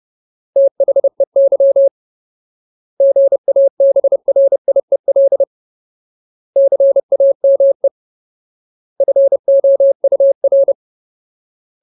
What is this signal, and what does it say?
Morse code, heard as a keyed tone, about 30 wpm, 550 Hz. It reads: THEY GABRIEL CAME FOUR
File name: Ultimate Word List / 30WPM Ultimate